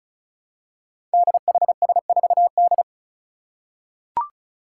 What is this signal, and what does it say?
Morse code, heard as a keyed tone, about 36 wpm, 700 Hz. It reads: DHS4D
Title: Five Challenging Letter-Number Combinations 35wpm